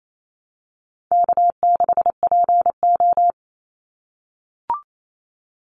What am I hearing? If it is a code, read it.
Morse code, heard as a keyed tone, about 28 wpm, 700 Hz. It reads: K6PO
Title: Easy Call Signs - 28wpm